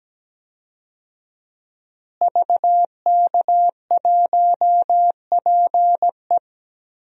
Morse code, heard as a keyed tone, about 17 wpm, 700 Hz. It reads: VK1PE
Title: DXCC-100 Call Signs - 5 Characters - Repeated 3x - 17wpm